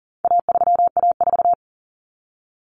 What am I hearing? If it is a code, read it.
Morse code, heard as a keyed tone, about 40 wpm, 700 Hz. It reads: A3A4
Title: Common Cut Numbers - 4-Digit 40wpm